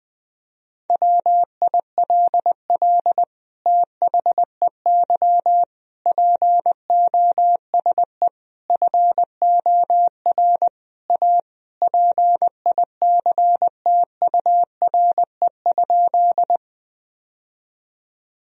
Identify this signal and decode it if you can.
Morse code, heard as a keyed tone, about 20 wpm, 700 Hz. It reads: WILL THEY POSE FOR A PICTURE?
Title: Sentences from Top 1000 Words - Review Part 2 - 20wpm